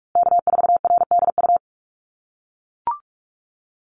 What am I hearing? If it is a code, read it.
Morse code, heard as a keyed tone, about 44 wpm, 700 Hz. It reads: K4RDU
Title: USA Call Signs Repeated 3x 45wpm